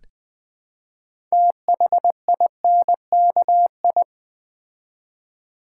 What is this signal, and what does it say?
Morse code, heard as a keyed tone, about 20 wpm, 700 Hz. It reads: THINKI
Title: Build A Word - 2 Letters At A Time - 8-Letter Words - 20wpm